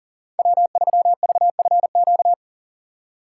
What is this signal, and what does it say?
Morse code, heard as a keyed tone, about 40 wpm, 700 Hz. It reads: W3VFQ
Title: USA Call Signs - 5 Characters - 2x Character Spacing - 40wpm